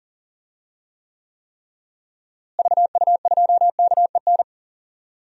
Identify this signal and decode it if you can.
Morse code, heard as a keyed tone, about 40 wpm, 700 Hz. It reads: VU2KEN
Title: DXCC-100 Call Signs - 6 Characters - Repeated 3x - 40wpm